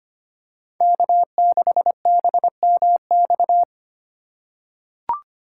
Morse code, heard as a keyed tone, about 25 wpm, 700 Hz. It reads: K6BMX
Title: USA Call Signs - 5 Characters - 2x Character Spacing - 25wpm